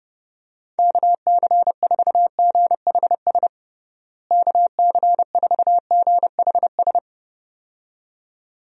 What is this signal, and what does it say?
Morse code, heard as a keyed tone, about 30 wpm, 700 Hz. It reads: KC4GHS KC4GHS
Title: USA Call Signs - 6 Characters - Repeated 2x - 30wpm